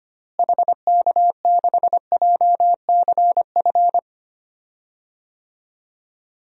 Morse code, heard as a keyed tone, about 25 wpm, 700 Hz. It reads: HK6JCF
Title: DXCC-100 Call Signs - 6 Characters - Repeated 2x - 25wpm